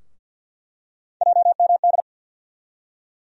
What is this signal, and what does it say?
Morse code, heard as a keyed tone, about 49 wpm, 690 Hz. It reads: JMD
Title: USA Call Sign Postfixes - 3 Letters 50wpm